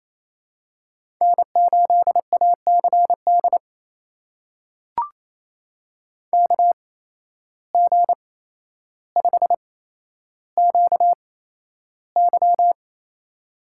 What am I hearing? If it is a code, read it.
Morse code, heard as a keyed tone, about 28 wpm, spaced as for 12 wpm, 700 Hz. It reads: N8ACD KG5QY
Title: USA Call Signs - 5 Characters - 8x Character Spacing - 28wpm